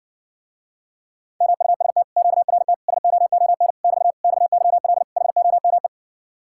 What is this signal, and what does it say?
Morse code, heard as a keyed tone, about 60 wpm, 700 Hz. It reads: MUST QRT SOON <BT> XYL HOME